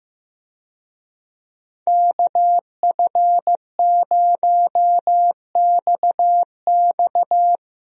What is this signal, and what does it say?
Morse code, heard as a keyed tone, about 15 wpm, 700 Hz. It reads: KF0XX
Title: USA Call Signs - 5 Characters - Repeated 2x - 15wpm